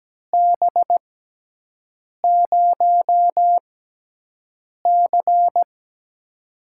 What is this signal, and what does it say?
Morse code, heard as a keyed tone, about 17 wpm, spaced as for 6 wpm, 700 Hz. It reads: B0C
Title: USA Call Signs - 4 Characters - 6x Character Spacing - 17wpm